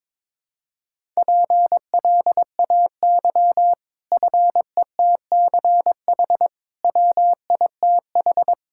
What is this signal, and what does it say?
Morse code, heard as a keyed tone, about 22 wpm, 700 Hz. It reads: PLAY FETCH WITH
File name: practice / 22wpm / Easy Sentences with Hints - 22wpm